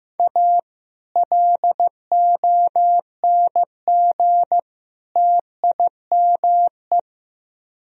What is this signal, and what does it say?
Morse code, heard as a keyed tone, about 17 wpm, 700 Hz. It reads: A LONG TIME
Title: Speed-Racing - Sets of 3 Words 15wpm